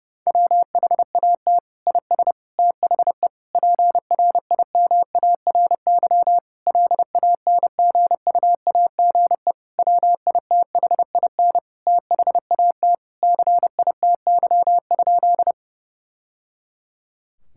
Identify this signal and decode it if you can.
Morse code, heard as a keyed tone, about 30 wpm, 700 Hz. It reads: WHAT IS THE PRIMARY LANGUAGE WITHIN THAT CITY?
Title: Sentences from Top 1000 Words - Review Part 5 - 30wpm